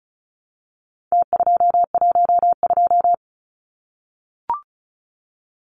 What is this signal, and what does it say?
Morse code, heard as a keyed tone, about 34 wpm, 700 Hz. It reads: T212
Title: Numbers - Contest Style - 4-Digit 35wpm